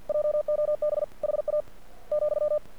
Morse code, heard as a keyed tone, about 48 wpm, 600 Hz. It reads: JOB HA ,
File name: archived / CRSnet